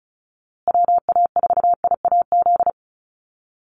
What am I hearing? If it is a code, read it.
Morse code, heard as a keyed tone, about 34 wpm, 700 Hz. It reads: WA4IAZ
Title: USA Call Signs Repeated 3x 35wpm